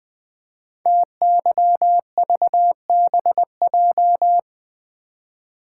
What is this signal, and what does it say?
Morse code, heard as a keyed tone, about 20 wpm, 700 Hz. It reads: TYVBJ
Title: Code Groups - 5 Letters - 20wpm